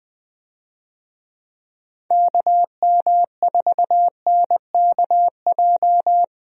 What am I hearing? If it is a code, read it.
Morse code, heard as a keyed tone, about 20 wpm, 700 Hz. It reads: KM4NKJ
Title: USA Call Signs - 6 Characters - Repeated 2x - 20wpm